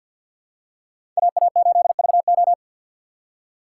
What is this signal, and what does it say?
Morse code, heard as a keyed tone, about 52 wpm, 700 Hz. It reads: AA8VO
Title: USA Call Signs - 5 Characters - 2x Character Spacing - 50wpm